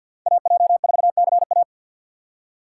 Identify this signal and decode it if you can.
Morse code, heard as a keyed tone, about 60 wpm, 700 Hz. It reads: AJ4CA
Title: Speed Racing - USA 2x2 Call Signs 50wpm